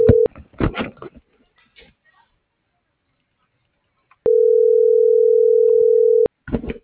Ringback tone (North America), 440 + 480 Hz.